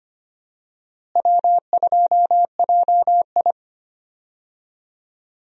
Morse code, heard as a keyed tone, about 25 wpm, 700 Hz. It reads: W2JI
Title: USA Call Signs - 4 Characters - Repeated 2x - 25wpm